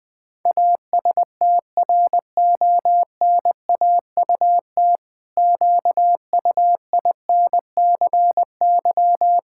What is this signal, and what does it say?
Morse code, heard as a keyed tone, about 20 wpm, 700 Hz. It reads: ASTRONAUT QUINCY
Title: Pangram Challenge v2 - 20wpm